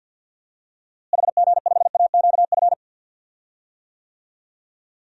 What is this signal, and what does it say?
Morse code, heard as a keyed tone, about 52 wpm, 700 Hz. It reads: HG5AQF
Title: DXCC-100 Call Signs - 6 Characters - Repeated 3x - 50wpm